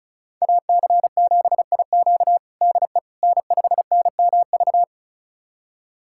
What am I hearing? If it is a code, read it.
Morse code, heard as a keyed tone, about 36 wpm, 700 Hz. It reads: AC7IQ DE N5NMV
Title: USA Callsign DE Callsign - 5 Characters - 35wpm